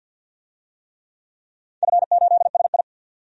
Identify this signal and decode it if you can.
Morse code, heard as a keyed tone, about 50 wpm, 700 Hz. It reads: F8SI
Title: DXCC-100 Call Signs - 4 Characters - 50wpm